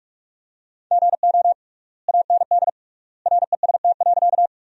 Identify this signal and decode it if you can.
Morse code, heard as a keyed tone, about 46 wpm, 700 Hz. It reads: GO AND REST.
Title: 3x Word-Spacing - Sentences from Top 500 Words - 45wpm